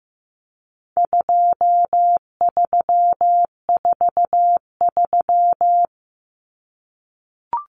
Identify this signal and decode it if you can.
Morse code, heard as a keyed tone, about 14 wpm, 700 Hz. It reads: 2343
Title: Numbers - 4-Digit 15wpm